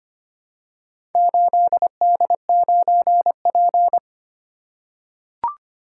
Morse code, heard as a keyed tone, about 25 wpm, 690 Hz. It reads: 8D9P
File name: practice / 25wpm / Four Letter-Number Combinations - 25wpm